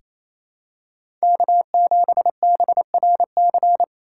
Morse code, heard as a keyed tone, about 28 wpm, 700 Hz. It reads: K7BRC
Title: USA Call Signs - 5 Characters - 2x Character Spacing - 28wpm